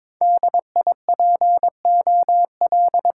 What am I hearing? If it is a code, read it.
Morse code, heard as a keyed tone, about 22 wpm, 700 Hz. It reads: DIPOL
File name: Top 90 QSO Elements - 6x Character Spacing - 22wpm